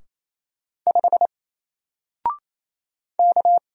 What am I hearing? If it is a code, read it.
Morse code, heard as a keyed tone, about 28 wpm, 700 Hz. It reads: 5 K
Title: Single Letter-Number Alt 28wpm